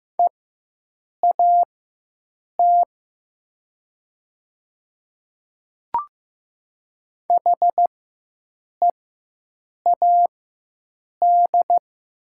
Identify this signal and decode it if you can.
Morse code, heard as a keyed tone, about 15 wpm, spaced as for 7 wpm, 700 Hz. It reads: EAT HEAD
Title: 4x Character Spacing - Top 300 Words - 15wpm